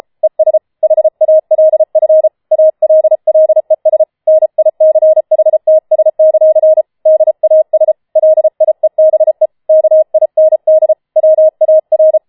Morse code, heard as a keyed tone, about 33 wpm, 630 Hz. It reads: ES HALF ALLES NICHTS; DAS LIEBE KIND WAR
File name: Schneewittchen Teil5 34wpm